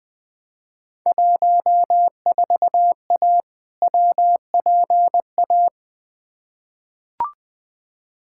Morse code, heard as a keyed tone, about 20 wpm, 700 Hz. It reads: 14A WPA
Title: Field Day Exchanges Encore - 20wpm